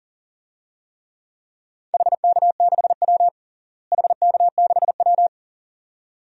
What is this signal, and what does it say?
Morse code, heard as a keyed tone, about 40 wpm, 700 Hz. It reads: HK6W HK6W
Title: DXCC-100 Call Signs - 4 Characters - Repeated 2x - 40wpm